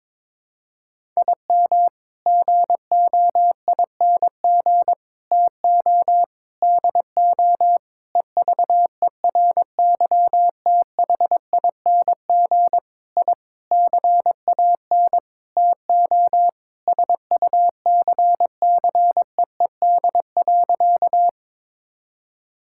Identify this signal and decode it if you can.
Morse code, heard as a keyed tone, about 21 wpm, 700 Hz. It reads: IM GOING TO DO EVERYTHING I CAN TO SUCCEED.
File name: practice / 22wpm / Sentences with 8-Word Phrases - 22wpm